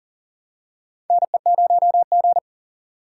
Morse code, heard as a keyed tone, about 40 wpm, 700 Hz. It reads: NE0G
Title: USA Call Signs - 4 Characters - 3x Character Spacing - 40wpm